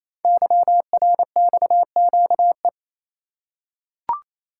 Morse code, heard as a keyed tone, about 27 wpm, 700 Hz. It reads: YRXQE
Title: Code Groups - 5 Letters - 28wpm